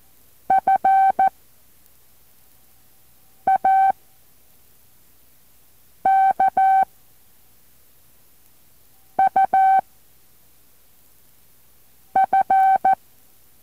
Morse code, heard as a keyed tone, about 14 wpm, 770 Hz. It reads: F A K U F
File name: intl morse code2